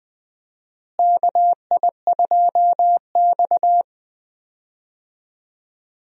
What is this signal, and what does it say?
Morse code, heard as a keyed tone, about 20 wpm, 700 Hz. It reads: KI2X
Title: USA Call Signs - 4 Characters - Repeated 3x - 20wpm